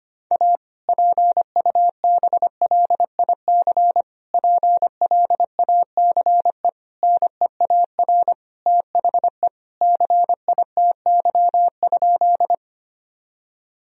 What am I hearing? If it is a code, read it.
Morse code, heard as a keyed tone, about 25 wpm, 700 Hz. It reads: A PUBLIC PLACE NEAR THE CITY?
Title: Sentences from Top 1000 Words - Review Part 3 - 25wpm